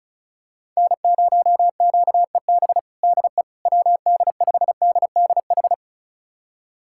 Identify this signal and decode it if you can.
Morse code, heard as a keyed tone, about 36 wpm, 700 Hz. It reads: N0QEB DE WD5DDH
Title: USA Callsign DE Callsign 35wpm